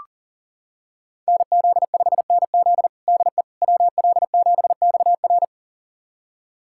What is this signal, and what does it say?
Morse code, heard as a keyed tone, about 37 wpm, 700 Hz. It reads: NZ5NZ DE WL7XR
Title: USA Callsign DE Callsign - 5 Characters - 40wpm